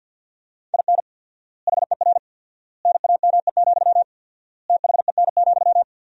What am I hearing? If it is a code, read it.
Morse code, heard as a keyed tone, about 52 wpm, 700 Hz. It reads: IN HER NAME, THEN,
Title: 4x Word-Spacing - Sentences from Top 500 Words - 50wpm